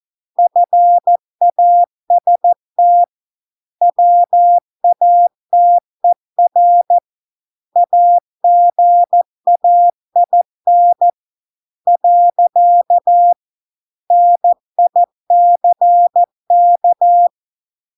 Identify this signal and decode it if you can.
Morse code, heard as a keyed tone, about 14 wpm, 700 Hz. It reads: FAST WATER AGAIN . NICK